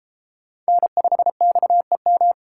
Morse code, heard as a keyed tone, about 29 wpm, 700 Hz. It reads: N5XEM